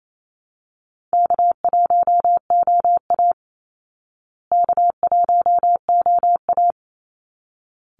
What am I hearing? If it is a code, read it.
Morse code, heard as a keyed tone, about 28 wpm, 700 Hz. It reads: K1OA K1OA
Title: Call Signs Repeated 3x - 28wpm